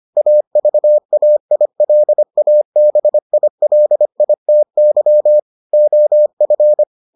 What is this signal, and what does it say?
Morse code, heard as a keyed tone, about 25 wpm, 600 Hz. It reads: AVAILABILITY OF